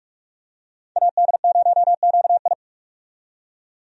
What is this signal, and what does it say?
Morse code, heard as a keyed tone, about 43 wpm, 700 Hz. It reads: AD0QI
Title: USA Call Signs - 5 Characters - 45wpm